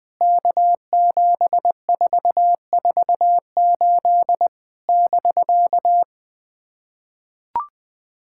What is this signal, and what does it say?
Morse code, heard as a keyed tone, about 20 wpm, 700 Hz. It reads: K7448 <BK>
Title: POTA QSO Practice - 20wpm